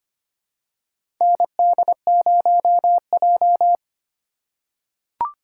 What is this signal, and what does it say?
Morse code, heard as a keyed tone, about 25 wpm, 700 Hz. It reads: ND0J
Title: USA Call Signs - 4x Character Spacing - 25wpm